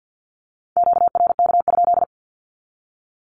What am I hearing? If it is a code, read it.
Morse code, heard as a keyed tone, about 50 wpm, 700 Hz. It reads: QRK?
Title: Q-Codes 50wpm